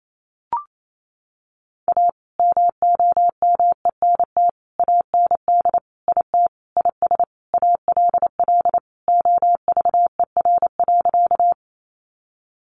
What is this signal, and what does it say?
Morse code, heard as a keyed tone, about 28 wpm, 700 Hz. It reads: A MOMENT AND IT IS ALL OVER.
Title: Sentences from Top 1000 Words 28wpm